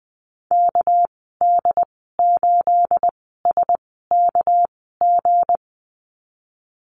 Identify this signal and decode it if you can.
Morse code, heard as a keyed tone, about 20 wpm, spaced as for 15 wpm, 700 Hz. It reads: KD8SKG
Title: 231 Lesson 37g - Letter x - Call Signs 20wpm